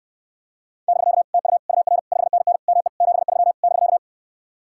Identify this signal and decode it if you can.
Morse code, heard as a keyed tone, about 77 wpm, 700 Hz. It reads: <BK> TU UR 5NN ME 73 <BK>